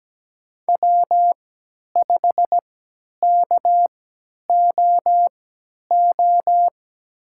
Morse code, heard as a keyed tone, about 17 wpm, spaced as for 10 wpm, 700 Hz. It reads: W5KOO
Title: USA Call Signs - 5 Characters - 3x Character Spacing - 17wpm